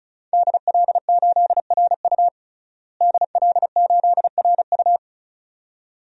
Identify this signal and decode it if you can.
Morse code, heard as a keyed tone, about 36 wpm, 700 Hz. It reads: DL8RU DL8RU
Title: DXCC-100 Call Signs - 5 Characters - Repeated 2x - 35wpm